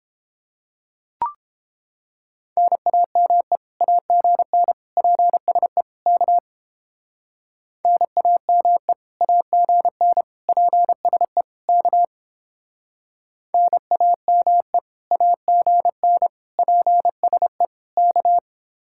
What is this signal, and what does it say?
Morse code, heard as a keyed tone, about 29 wpm, 700 Hz. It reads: NAME AGN PSE K NAME AGN PSE K NAME AGN PSE K